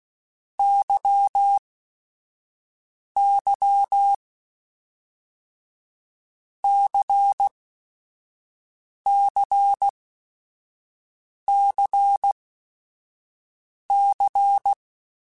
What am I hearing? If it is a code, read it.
Morse code, heard as a keyed tone, about 16 wpm, 800 Hz. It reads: Y Y C C C C
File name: Basic Sets / Basic Letter Groups Collective7 / LFCYQ@5